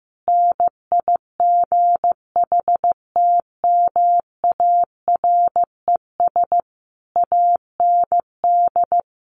Morse code, heard as a keyed tone, about 15 wpm, 700 Hz. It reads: NIGHTMARES AND
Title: A Sound Of Thunder 15wpm